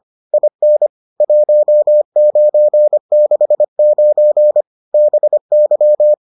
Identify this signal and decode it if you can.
Morse code, heard as a keyed tone, about 25 wpm, 600 Hz. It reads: IN 1969 BY